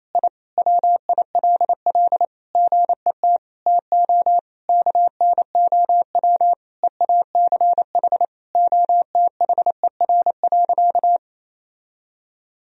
Morse code, heard as a keyed tone, about 28 wpm, 700 Hz. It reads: I WILL GET TO KNOW EACH OTHER.
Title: Sentences from Top 1000 Words - Review Part 3 - 28wpm